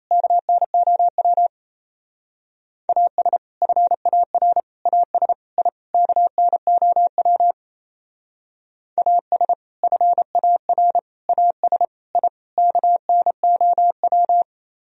Morse code, heard as a keyed tone, about 36 wpm, 700 Hz. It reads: KNOW AS FAR AS I KNOW AS FAR AS I KNOW